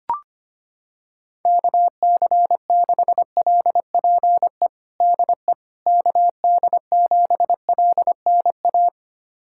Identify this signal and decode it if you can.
Morse code, heard as a keyed tone, about 25 wpm, 700 Hz. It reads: KC6LPE DE KD7LNA
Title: USA Callsign DE Callsign - 6 Characters - 25wpm